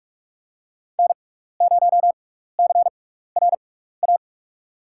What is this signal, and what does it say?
Morse code, heard as a keyed tone, about 43 wpm, spaced as for 15 wpm, 700 Hz. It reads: N0CRA
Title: USA Call Signs - 5 Characters - 6x Character Spacing - 45wpm